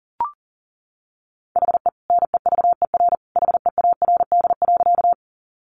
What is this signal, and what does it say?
Morse code, heard as a keyed tone, about 40 wpm, 700 Hz. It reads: HE NEVER HEARD.
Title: Sentences from Top 400 Words 40wpm